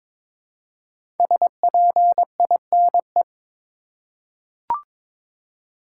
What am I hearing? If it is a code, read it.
Morse code, heard as a keyed tone, about 22 wpm, 700 Hz. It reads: SPINE
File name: 5 Letter Words - First 3 Letters Given - 22wpm